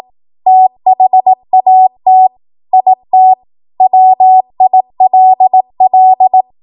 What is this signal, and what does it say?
Morse code, heard as a keyed tone, about 18 wpm, 750 Hz. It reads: THAT IT WILL
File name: Morse / 18WPM